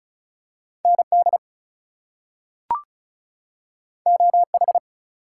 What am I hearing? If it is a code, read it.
Morse code, heard as a keyed tone, about 36 wpm, 700 Hz. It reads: ND OH
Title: US States and Canadian Provinces 35wpm